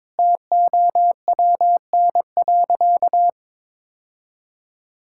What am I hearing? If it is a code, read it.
Morse code, heard as a keyed tone, about 21 wpm, 700 Hz. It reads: TOWN.
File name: practice / 22wpm / Sentences with 3-Word Phrases - 22wpm